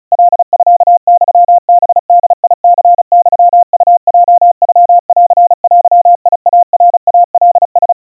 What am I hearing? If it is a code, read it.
Morse code (Japanese Wabun code), heard as a keyed tone, about 35 wpm, 700 Hz. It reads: カミユハボニユウヲノンデイナイカラ
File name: kawa / Morse sample / kamyu 130 cpm 700 Hz Crispy